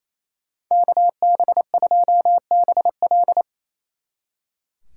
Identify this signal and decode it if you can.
Morse code, heard as a keyed tone, about 33 wpm, 700 Hz. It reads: KB2BL
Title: Speed Racing - USA Call Signs 28wpm